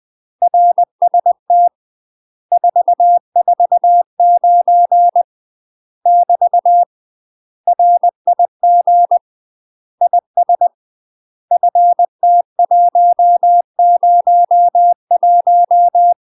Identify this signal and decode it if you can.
Morse code, heard as a keyed tone, about 20 wpm, 700 Hz. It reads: RST 449 <BT> RIG IS FT101
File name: UK MASTER QSOs TRAINING / 20WPM / UK MASTER QSOs 0070